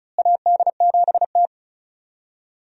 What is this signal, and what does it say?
Morse code, heard as a keyed tone, about 34 wpm, 700 Hz. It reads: AD7T
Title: USA Call Signs - 4 Characters - Repeated 2x - 35wpm